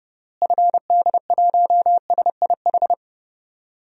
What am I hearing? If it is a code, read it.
Morse code, heard as a keyed tone, about 30 wpm, 700 Hz. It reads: FD1SIH
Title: DXCC-100 Call Signs - 6 Characters - Repeated 3x - 30wpm